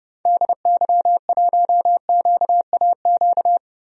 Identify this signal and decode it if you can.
Morse code, heard as a keyed tone, about 30 wpm, 700 Hz. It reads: DY1QAQ
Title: DXCC-100 Call Signs - 6 Characters - Repeated 2x - 30wpm